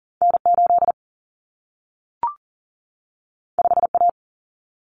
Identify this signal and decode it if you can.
Morse code, heard as a keyed tone, about 40 wpm, 700 Hz. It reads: N8 5A
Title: Common Cut Numbers - 2-Digit 40wpm